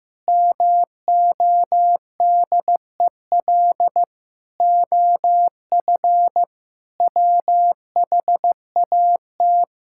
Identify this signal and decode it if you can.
Morse code, heard as a keyed tone, about 15 wpm, 700 Hz. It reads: MODEL OF WHAT
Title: Sentences from Top 1000 Words - Review Part 5 - 15wpm